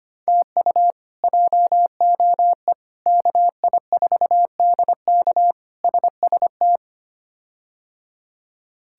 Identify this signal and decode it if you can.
Morse code, heard as a keyed tone, about 25 wpm, 700 Hz. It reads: TU JOE KI4DK SST
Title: K1USN SST QSO Practice - 25wpm